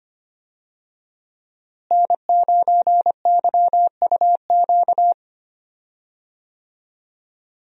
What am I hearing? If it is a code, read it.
Morse code, heard as a keyed tone, about 25 wpm, 700 Hz. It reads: N9YUQ
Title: USA Call Signs - 5 Characters - Repeated 2x - 25wpm